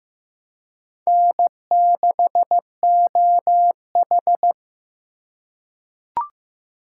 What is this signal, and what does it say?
Morse code, heard as a keyed tone, about 15 wpm, 700 Hz. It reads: N6OH
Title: USA Call Signs - 4 Characters - 3x Character Spacing - 15wpm